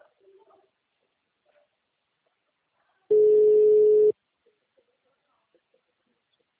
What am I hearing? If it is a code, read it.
Ringback tone (Europe, 425 Hz), with speech.
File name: out- -2017- - - .62332